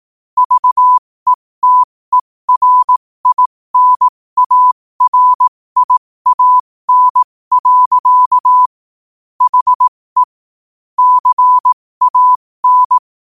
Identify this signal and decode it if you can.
Morse code, heard as a keyed tone, about 18 wpm, spaced as for 15 wpm, 1000 Hz. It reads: VETERINARIAN. HE CAN
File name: qotdfiles / 15wpm